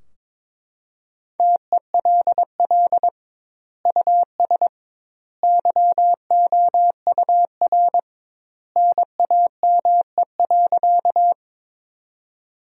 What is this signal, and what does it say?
Morse code, heard as a keyed tone, about 22 wpm, 700 Hz. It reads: TELL US YOUR NAME.
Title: 2x Word-Spacing - Sentences from Top 200 Words - 22wpm